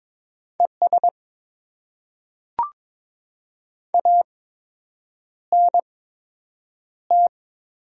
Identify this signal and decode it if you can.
Morse code, heard as a keyed tone, about 21 wpm, spaced as for 8 wpm, 700 Hz. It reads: ES ANT
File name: Top 20 QSO Elements - 8x Character Spacing - 22wpm